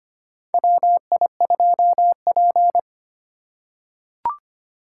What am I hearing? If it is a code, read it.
Morse code, heard as a keyed tone, about 25 wpm, 700 Hz. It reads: WI2P
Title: USA Call Signs - 4x Character Spacing - 25wpm